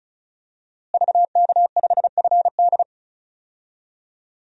Morse code, heard as a keyed tone, about 34 wpm, 700 Hz. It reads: VK5FD
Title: DXCC-100 Call Signs - 5 Characters - Repeated 3x - 35wpm